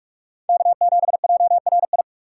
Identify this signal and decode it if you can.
Morse code, heard as a keyed tone, about 44 wpm, 700 Hz. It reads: K7JRI